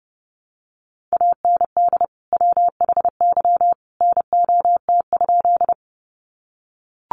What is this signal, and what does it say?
Morse code, heard as a keyed tone, about 30 wpm, 700 Hz. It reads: AND WHY NOT?
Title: Sentences from Top 600 Words 30wpm